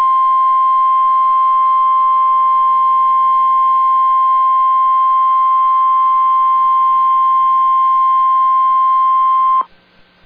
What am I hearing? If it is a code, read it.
Emergency Alert System (SAME) tones (the 1050 Hz attention tone). It is heard from 1.2 to 9.7 s.